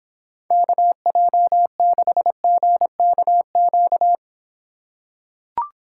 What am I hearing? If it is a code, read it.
Morse code, heard as a keyed tone, about 26 wpm, 700 Hz. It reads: KJ6GKQ